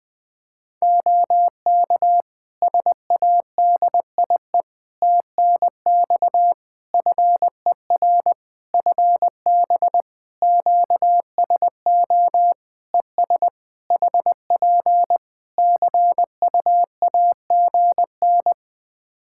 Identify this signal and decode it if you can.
Morse code, heard as a keyed tone, about 19 wpm, 700 Hz. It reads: OK SADIE TNX FER FB QSO ES HP CUAGN
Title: Practice QSO - Exchange 4 - Ending - 20wpm